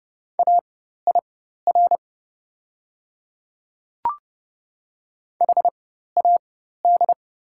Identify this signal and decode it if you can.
Morse code, heard as a keyed tone, about 30 wpm, spaced as for 14 wpm, 700 Hz. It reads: AIR HAD
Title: 4x Character Spacing - Top 200 Words - 30wpm